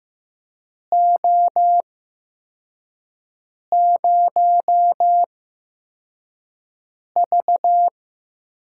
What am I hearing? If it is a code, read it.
Morse code, heard as a keyed tone, about 15 wpm, spaced as for 4 wpm, 700 Hz. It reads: O0V
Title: USA Call Signs - 8x Character Spacing - 15wpm